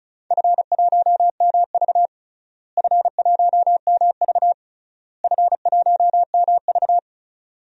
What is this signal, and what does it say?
Morse code, heard as a keyed tone, about 36 wpm, 690 Hz. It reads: F1MV F1MV F1MV
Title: DXCC-100 Call Signs - 4 Characters - Repeated 3x - 35wpm